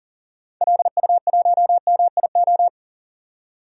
Morse code, heard as a keyed tone, about 40 wpm, 700 Hz. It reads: LU1MIO
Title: DXCC-100 Call Signs - 6 Characters - 40wpm